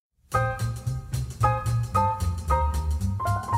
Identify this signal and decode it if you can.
There is no code.